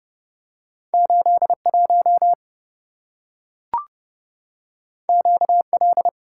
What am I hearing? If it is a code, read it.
Morse code, heard as a keyed tone, about 27 wpm, 700 Hz. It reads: 81 QL